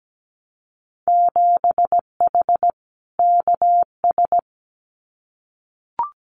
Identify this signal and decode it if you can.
Morse code, heard as a keyed tone, about 17 wpm, 700 Hz. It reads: 7H KS
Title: Winter Field Day Exchanges Encore 17wpm